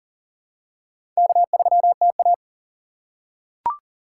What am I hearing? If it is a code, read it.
Morse code, heard as a keyed tone, about 40 wpm, 700 Hz. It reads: K3TA
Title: USA Call Signs - 4 Characters - 6x Character Spacing - 40wpm